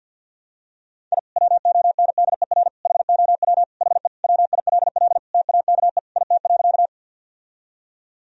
Morse code, heard as a keyed tone, about 50 wpm, 700 Hz. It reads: I WONDER HOW HE WILL TAKE IT.
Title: Sentences from Top 1000 Words - Review Part 3 - 50wpm